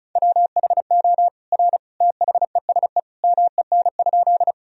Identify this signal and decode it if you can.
Morse code, heard as a keyed tone, about 36 wpm, 690 Hz. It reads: WHO R THESE MEN?
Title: Sentences from Top 200 Words and Abbreviations 35wpm